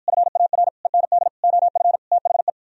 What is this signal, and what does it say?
Morse code, heard as a keyed tone, about 52 wpm, 700 Hz. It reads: FAR END OF THE